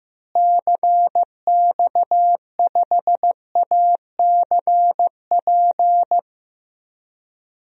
Morse code, heard as a keyed tone, about 15 wpm, 700 Hz. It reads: CX5ACP
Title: DXCC-100 Call Signs - 6 Characters - Repeated 2x - 15wpm